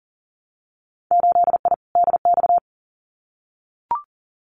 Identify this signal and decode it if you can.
Morse code, heard as a keyed tone, about 38 wpm, 700 Hz. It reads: 8I DX
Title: Winter Field Day Exchanges Encore 40wpm